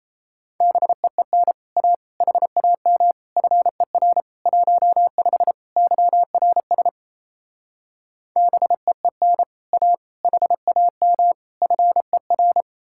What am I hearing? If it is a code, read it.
Morse code, heard as a keyed tone, about 37 wpm, 700 Hz. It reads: BEEN A HAM FER 15 YRS BEEN A HAM FER